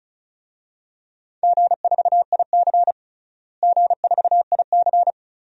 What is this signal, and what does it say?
Morse code, heard as a keyed tone, about 34 wpm, 700 Hz. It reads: G4IC G4IC
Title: DXCC-100 Call Signs - 4 Characters - Repeated 2x - 35wpm